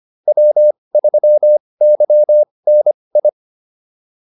Morse code, heard as a keyed tone, about 25 wpm, spaced as for 20 wpm, 600 Hz. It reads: W3YNI